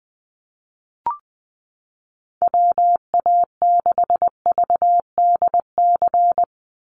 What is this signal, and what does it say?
Morse code, heard as a keyed tone, about 20 wpm, 700 Hz. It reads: WA6VDC
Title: USA Call Signs Repeated 2x 20wpm